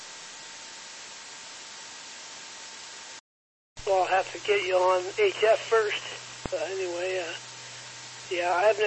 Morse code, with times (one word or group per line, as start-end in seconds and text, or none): none